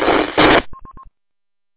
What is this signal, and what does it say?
There is no code.